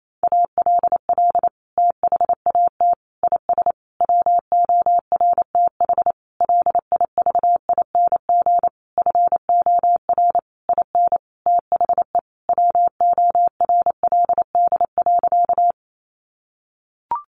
Morse code, heard as a keyed tone, about 28 wpm, 700 Hz. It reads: ALL THAT IS WORTH LIVING FOR IN THE WORLD.
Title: Sentences from Top 1100 Words 28wpm